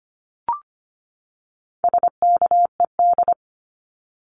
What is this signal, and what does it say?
Morse code, heard as a keyed tone, about 25 wpm, 700 Hz. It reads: SKED
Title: Abbreviations 25wpm